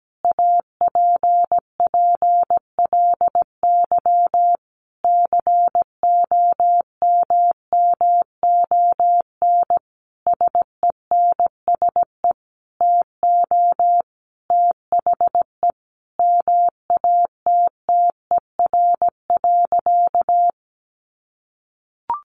Morse code, heard as a keyed tone, about 17 wpm, 700 Hz. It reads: APPLY COMMON SENSE TO THE MATTER.
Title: Sentences from Top 1200 Words 17wpm